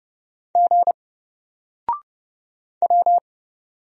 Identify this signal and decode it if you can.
Morse code, heard as a keyed tone, about 29 wpm, 700 Hz. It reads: G W
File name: Single-Letter-Number-30wpm